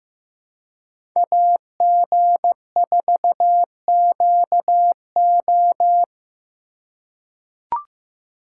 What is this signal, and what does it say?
Morse code, heard as a keyed tone, about 15 wpm, 700 Hz. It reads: AG4QO
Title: USA Call Signs - 5 Characters - 4x Character Spacing - 15wpm